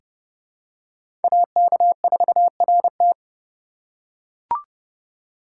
Morse code, heard as a keyed tone, about 30 wpm, 700 Hz. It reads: AK4RT
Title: USA Call Signs - 2x Character Spacing - 30wpm